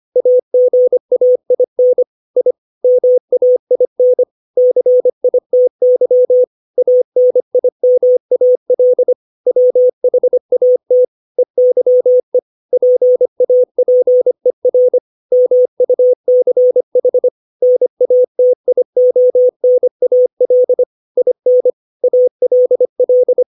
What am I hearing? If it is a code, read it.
Morse code, heard as a keyed tone, about 25 wpm, 500 Hz. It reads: AGAIN I MAIN CITY ANIMAL WHAT EYE PAPER MUCH NATIONAL IN ALL
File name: 632 words / 632 Words 25WPM